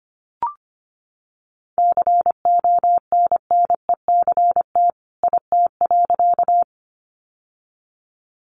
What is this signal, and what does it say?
Morse code, heard as a keyed tone, about 25 wpm, 700 Hz. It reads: CONNECT IT.
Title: Sentences from Top 1500 Words 25wpm